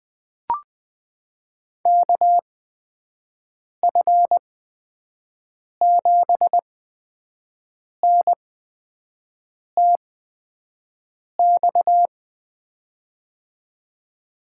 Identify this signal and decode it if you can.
Morse code, heard as a keyed tone, about 21 wpm, spaced as for 5 wpm, 700 Hz. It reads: KF7N TX
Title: USA Call Signs - 8x Character Spacing - 20wpm